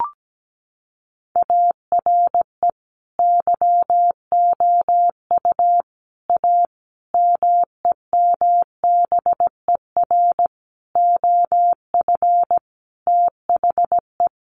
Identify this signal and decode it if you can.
Morse code, heard as a keyed tone, about 17 wpm, 700 Hz. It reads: ARE YOU A MEMBER OF THE
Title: Sentences from Top 1400 Words 17wpm